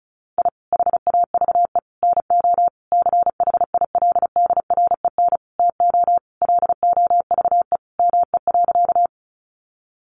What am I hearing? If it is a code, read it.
Morse code, heard as a keyed tone, about 34 wpm, 700 Hz. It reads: I HAVE NO CHILDREN TO LOVE ME.
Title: Sentences from Top 400 Words 35wpm